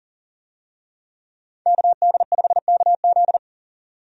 Morse code, heard as a keyed tone, about 40 wpm, 700 Hz. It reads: KD5KZ
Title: USA Call Signs - 5 Characters - 40wpm